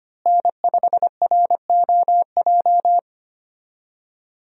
Morse code, heard as a keyed tone, about 25 wpm, 700 Hz. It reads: N5ROJ
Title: USA Call Signs - 5 Characters - Repeated 2x - 25wpm